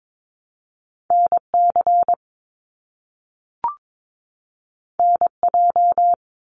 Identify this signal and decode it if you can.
Morse code, heard as a keyed tone, about 22 wpm, 700 Hz. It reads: NC NJ
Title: US States 22wpm